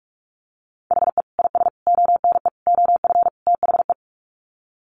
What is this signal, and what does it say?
Morse code, heard as a keyed tone, about 44 wpm, 700 Hz. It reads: HE IS ONE OF THE
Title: Sets of 5 Words Encore 45wpm